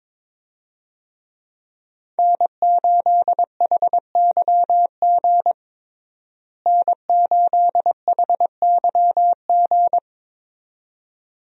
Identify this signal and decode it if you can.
Morse code, heard as a keyed tone, about 22 wpm, 700 Hz. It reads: N8HYG N8HYG
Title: USA Call Signs - 5 Characters - Repeated 2x - 22wpm